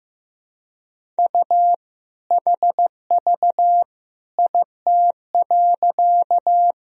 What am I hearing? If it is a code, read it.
Morse code, heard as a keyed tone, about 15 wpm, 700 Hz. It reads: U HV IT.
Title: Sentences from Top 100 Words and Abbreviations 15wpm